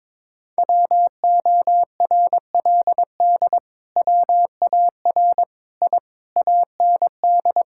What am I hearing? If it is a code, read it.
Morse code, heard as a keyed tone, about 22 wpm, 700 Hz. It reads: WORLD WAR I AND
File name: Sentences with 7-Word Phrases - 22wpm